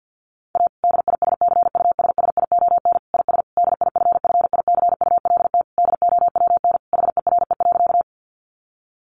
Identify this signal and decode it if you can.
Morse code, heard as a keyed tone, about 50 wpm, 700 Hz. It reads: A DISCUSSION IS DIFFICULT DOWN HERE.
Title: Sentences from Top 1100 Words 50wpm